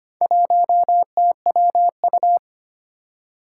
Morse code, heard as a keyed tone, about 25 wpm, 700 Hz. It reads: 1TWU
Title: Four Challenging Letter-Number Combinations 25wpm